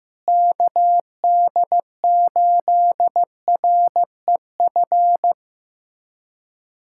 Morse code, heard as a keyed tone, about 15 wpm, 700 Hz. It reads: KD8REF
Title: USA Callsign DE Callsign 15wpm